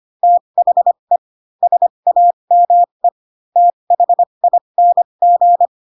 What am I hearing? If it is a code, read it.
Morse code, heard as a keyed tone, about 25 wpm, spaced as for 22 wpm, 700 Hz. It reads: THE SAME THING
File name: Pinocchio / Pinocchio ch4 18 25 ik1hge